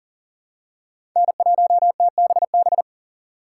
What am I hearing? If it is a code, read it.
Morse code, heard as a keyed tone, about 40 wpm, 700 Hz. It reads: N1TBB
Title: USA Call Signs - 4x Character Spacing - 40wpm